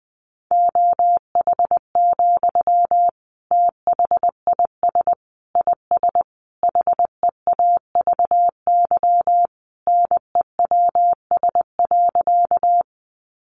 Morse code, heard as a keyed tone, about 20 wpm, 700 Hz. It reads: OH, THIS IS HEAVY NEWS.
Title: Sentences from Top 1100 Words 20wpm